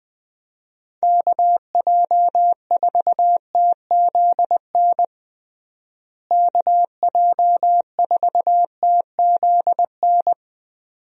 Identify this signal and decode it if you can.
Morse code, heard as a keyed tone, about 20 wpm, 700 Hz. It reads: KJ4TZN KJ4TZN
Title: USA Call Signs - 6 Characters - Repeated 2x - 20wpm